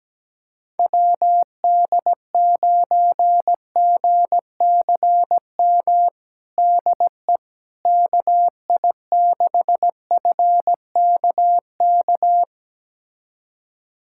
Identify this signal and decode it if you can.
Morse code, heard as a keyed tone, about 17 wpm, 700 Hz. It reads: WD9GCM DE KI6FKK
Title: USA Callsign DE Callsign - 6 Characters - 17wpm